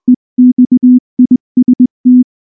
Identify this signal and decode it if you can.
Morse code, heard as a keyed tone, about 19 wpm, 250 Hz. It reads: EXIST
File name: morse / morse-training-20WPM-7